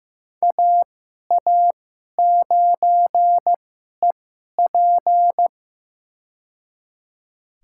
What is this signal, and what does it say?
Morse code, heard as a keyed tone, about 15 wpm, spaced as for 11 wpm, 700 Hz. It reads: AA9EP
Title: USA Call Signs - 5 Characters - 2x Character Spacing - 15wpm